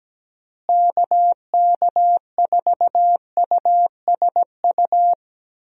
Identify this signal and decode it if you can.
Morse code, heard as a keyed tone, about 17 wpm, 700 Hz. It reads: KK4USU
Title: USA Call Signs - 6 Characters - Repeated 3x - 17wpm